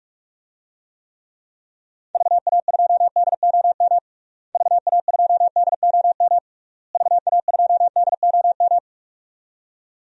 Morse code, heard as a keyed tone, about 46 wpm, 700 Hz. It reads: VA2DOM VA2DOM VA2DOM
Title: DXCC-100 Call Signs - 6 Characters - Repeated 3x - 45wpm